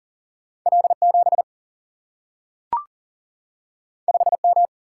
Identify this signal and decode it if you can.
Morse code, heard as a keyed tone, about 40 wpm, 700 Hz. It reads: L7 5M
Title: Single-Letter Single-Number 40wpm